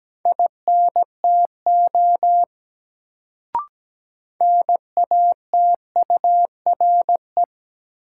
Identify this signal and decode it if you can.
Morse code, heard as a keyed tone, about 17 wpm, 700 Hz. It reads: INTO NATURE
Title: Sentences from Top 600 Words - New Word Drill - 17wpm